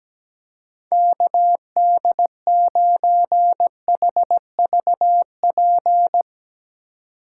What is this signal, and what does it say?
Morse code, heard as a keyed tone, about 17 wpm, 700 Hz. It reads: KD9HVP
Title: Speed Racing - USA 2x3 Call Signs 17wpm